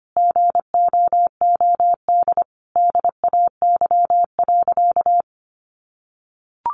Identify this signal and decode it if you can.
Morse code, heard as a keyed tone, about 25 wpm, 700 Hz. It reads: GOOD DAY.
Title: Sentences from Top 600 Words 25wpm